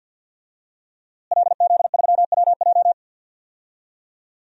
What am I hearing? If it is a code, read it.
Morse code, heard as a keyed tone, about 52 wpm, 700 Hz. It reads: LZ3WJ
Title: DXCC-100 Call Signs - 5 Characters - Repeated 2x - 50wpm